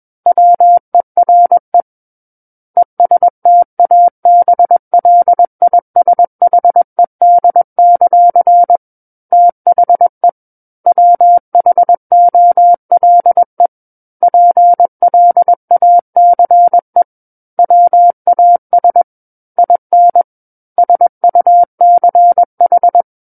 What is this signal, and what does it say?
Morse code, heard as a keyed tone, about 21 wpm, 700 Hz. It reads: WERE ESTABLISHED; THE WHOLE PLACE WAS IN SUCH
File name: Book One / 12-Chapter 12 - 21 WPM